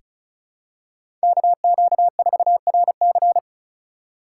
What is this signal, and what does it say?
Morse code, heard as a keyed tone, about 36 wpm, 700 Hz. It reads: KQ4RC
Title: USA Call Signs - 5 Characters - 2x Character Spacing - 35wpm